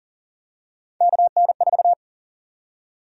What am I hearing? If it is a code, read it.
Morse code, heard as a keyed tone, about 40 wpm, 700 Hz. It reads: KN4
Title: USA Call Sign Prefixes - 40wpm